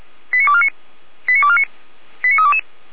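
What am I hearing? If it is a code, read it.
Five-tone selective call (ZVEI), heard as quick runs of five tones, tones of about 70 ms. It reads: 89249 89249 89322
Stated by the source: ZVEI five-tone selective call 89230 received on an SDR, served by Feuerwehr Orlamünde.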